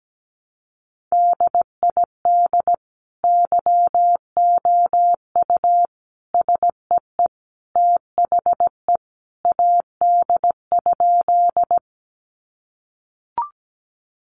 Morse code, heard as a keyed tone, about 17 wpm, 700 Hz. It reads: DID YOU SEE THE AD?
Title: Sentences from Top 1300 Words 17wpm